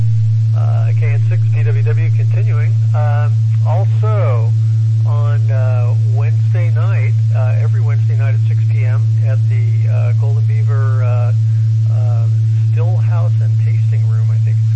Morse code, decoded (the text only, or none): none